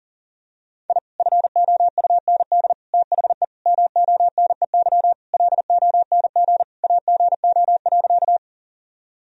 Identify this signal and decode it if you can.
Morse code, heard as a keyed tone, about 37 wpm, 700 Hz. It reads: I FOUND THE MONEY LONG AGO.
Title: Sentences from Top 1000 Words - Review Part 3 - 40wpm